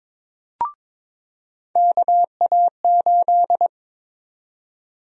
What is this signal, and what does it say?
Morse code, heard as a keyed tone, about 22 wpm, 700 Hz. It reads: KA8
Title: USA Call Sign Prefixes - 22wpm